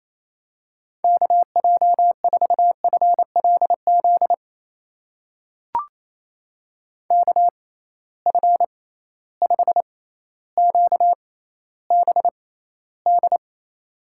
Morse code, heard as a keyed tone, about 27 wpm, spaced as for 14 wpm, 700 Hz. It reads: KJ4FLZ KF5QBD
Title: USA Call Signs - 6x Character Spacing - 28wpm